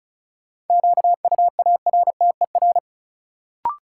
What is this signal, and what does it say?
Morse code, heard as a keyed tone, about 34 wpm, 700 Hz. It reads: QUARTER
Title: Sentences from Top 1300 Words - New Word Drill Alt - 35wpm